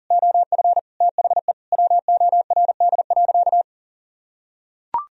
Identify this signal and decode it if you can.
Morse code, heard as a keyed tone, about 40 wpm, 700 Hz. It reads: OF THE WORD.
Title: Sentences from Top 1000 Words - Review Part 4 - 40wpm